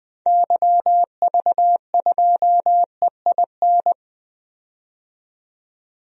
Morse code, heard as a keyed tone, about 20 wpm, 700 Hz. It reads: YV2EIN
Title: DXCC-100 Call Signs - 6 Characters - 20wpm